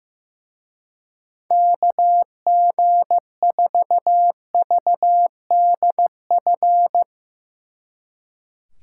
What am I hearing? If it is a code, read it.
Morse code, heard as a keyed tone, about 18 wpm, 700 Hz. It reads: KG4VDF
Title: Speed Racing - USA 2x3 Call Signs 15wpm